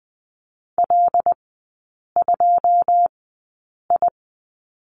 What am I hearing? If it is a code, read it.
Morse code, heard as a keyed tone, about 20 wpm, spaced as for 8 wpm, 700 Hz. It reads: L2I
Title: 159 Lesson 26e - Letter g - Three Characters 20wpm